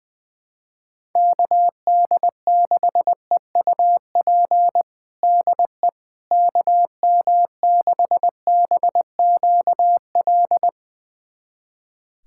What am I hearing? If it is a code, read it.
Morse code, heard as a keyed tone, about 20 wpm, 700 Hz. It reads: KD6EUP DE KM6BQL
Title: USA Callsign DE Callsign - 6 Characters - 20wpm